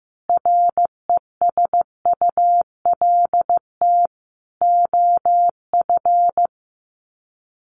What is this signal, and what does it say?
Morse code, heard as a keyed tone, about 14 wpm, 700 Hz. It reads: RESULT OF
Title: Sets of 4 Words 15wpm